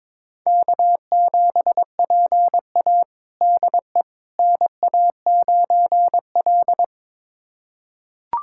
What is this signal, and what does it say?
Morse code, heard as a keyed tone, about 22 wpm, 700 Hz. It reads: K7PA DE NA9L
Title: USA Callsign DE Callsign - 4 Characters - 22wpm